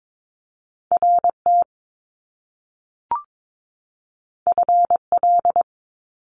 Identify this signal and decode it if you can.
Morse code, heard as a keyed tone, about 22 wpm, 700 Hz. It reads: RT FL
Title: Top Two Letter Combinations in Words 22wpm